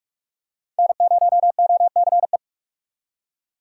Morse code, heard as a keyed tone, about 47 wpm, 700 Hz. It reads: N0OCE
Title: USA Call Signs - 3x Character Spacing - 45wpm